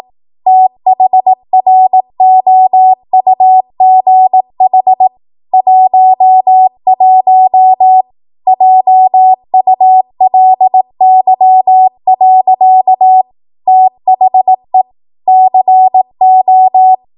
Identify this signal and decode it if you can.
Morse code, heard as a keyed tone, about 18 wpm, 750 Hz. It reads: THROUGH 11 JULY. THE CO
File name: Morse / 18WPM